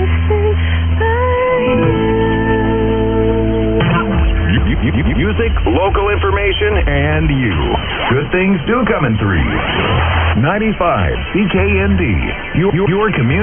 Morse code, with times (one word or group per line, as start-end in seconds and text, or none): none